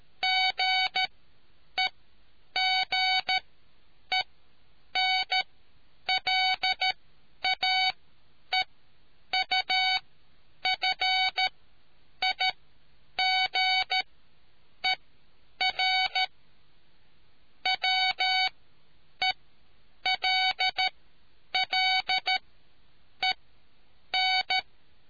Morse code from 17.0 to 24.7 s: WELLEN